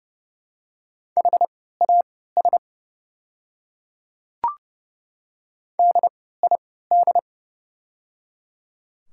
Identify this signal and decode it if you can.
Morse code, heard as a keyed tone, about 30 wpm, spaced as for 17 wpm, 700 Hz. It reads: HAS DID
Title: 3x Character Spacing - Top 100 Words - 30wpm